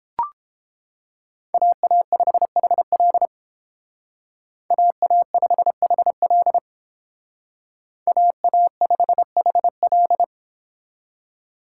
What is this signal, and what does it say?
Morse code, heard as a keyed tone, about 30 wpm, 700 Hz. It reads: AA5HL AA5HL AA5HL